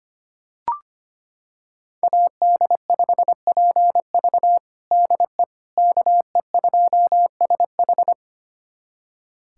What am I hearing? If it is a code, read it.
Morse code, heard as a keyed tone, about 25 wpm, 700 Hz. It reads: AD5PV DE KE2SH
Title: USA Callsign DE Callsign - 5 Characters - 25wpm